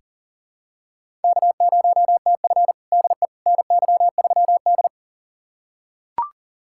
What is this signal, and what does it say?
Morse code, heard as a keyed tone, about 37 wpm, 700 Hz. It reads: K0TF DE NY3D
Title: USA Callsign DE Callsign - 4 Characters - 40wpm